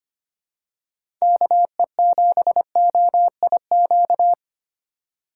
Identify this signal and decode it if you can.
Morse code, heard as a keyed tone, about 25 wpm, 700 Hz. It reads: KE7OIQ
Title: USA Call Signs - 6 Characters - 2x Character Spacing - 25wpm